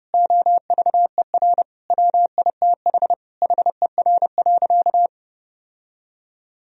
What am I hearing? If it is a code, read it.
Morse code, heard as a keyed tone, about 32 wpm, 700 Hz. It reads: OVER WITH HER.
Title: Sentences from Top 1000 Words - Review Part 2 - 30wpm